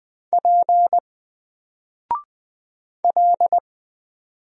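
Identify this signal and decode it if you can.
Morse code, heard as a keyed tone, about 20 wpm, 700 Hz. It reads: P L
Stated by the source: Single letter and number Morse drill at 20 wpm, from N4LCD.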